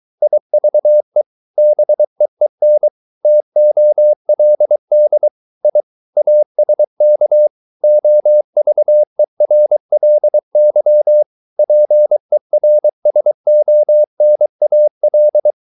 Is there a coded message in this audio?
Yes: Morse code from 0.2 to 15.5 s, heard as a keyed tone, about 23 wpm, 600 Hz. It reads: IVE BEEN TOLD I ASK OVERLY PERSONAL